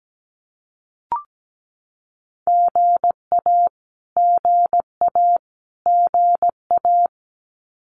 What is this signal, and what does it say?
Morse code, heard as a keyed tone, about 17 wpm, 700 Hz. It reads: GA GA GA
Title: Abbreviations Repeated 3x - 17wpm